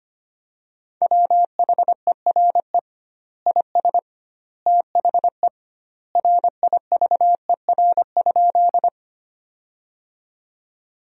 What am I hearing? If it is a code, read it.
Morse code, heard as a keyed tone, about 25 wpm, 700 Hz. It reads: WHERE IS THE RIVER?
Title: 2x Word-Spacing - Sentences from Top 300 Words - 25wpm